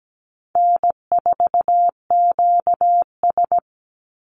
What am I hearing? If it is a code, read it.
Morse code, heard as a keyed tone, about 17 wpm, 700 Hz. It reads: N4QS
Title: Easy Call Signs Repeated 2x - 17wpm